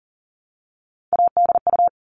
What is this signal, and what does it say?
Morse code, heard as a keyed tone, about 40 wpm, 700 Hz. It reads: ADU
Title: All Cut Numbers - 3-Digit 40wpm